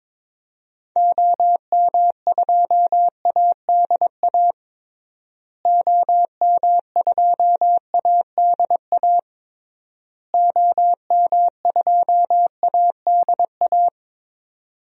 Morse code, heard as a keyed tone, about 22 wpm, 700 Hz. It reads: OM2ADA OM2ADA OM2ADA
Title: DXCC-100 Call Signs - 6 Characters - Repeated 3x - 22wpm